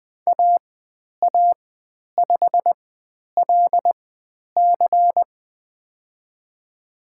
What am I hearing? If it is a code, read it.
Morse code, heard as a keyed tone, about 20 wpm, spaced as for 10 wpm, 700 Hz. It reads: AA5LC
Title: 095 Lesson 16g - Letter c - Call Signs 20wpm